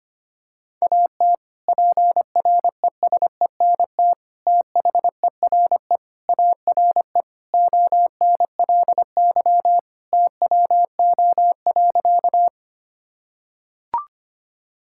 Morse code, heard as a keyed tone, about 25 wpm, 700 Hz. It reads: AT PRESENT THERE ARE ONLY TWO.
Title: Sentences from Top 1000 Words - Review Part 5 - 25wpm